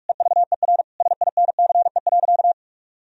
Morse code, heard as a keyed tone, about 48 wpm, 700 Hz. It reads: EVER SINCE.
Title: Sentences from Top 1000 Words - Review Part 3 - 45wpm